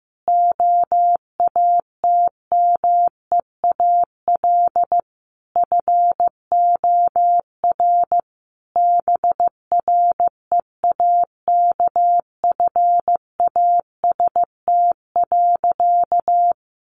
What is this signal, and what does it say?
Morse code, heard as a keyed tone, about 15 wpm, 700 Hz. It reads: OATMEAL FOR BREAKFAST.
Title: Easy Phrases 15wpm